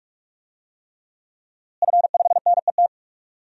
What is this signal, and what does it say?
Morse code, heard as a keyed tone, about 46 wpm, 700 Hz. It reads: F5NET
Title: DXCC-100 Call Signs - 5 Characters - Repeated 3x - 45wpm